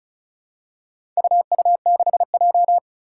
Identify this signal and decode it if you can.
Morse code, heard as a keyed tone, about 36 wpm, 700 Hz. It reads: UU6J
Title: Four Challenging Letter-Number Combinations 35wpm